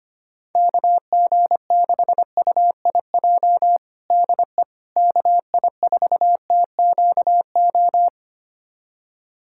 Morse code, heard as a keyed tone, about 25 wpm, 700 Hz. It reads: KG6UIJ DE KI4TQO
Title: USA Callsign DE Callsign - 6 Characters - 25wpm